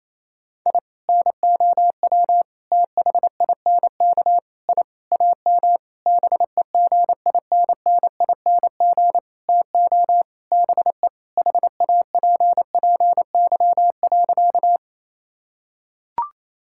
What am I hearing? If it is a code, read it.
Morse code, heard as a keyed tone, about 28 wpm, 700 Hz. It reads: I NOW THINK I AM BEGINNING TO BE HAPPY.
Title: Sentences from Top 1000 Words - Review Part 5 - 28wpm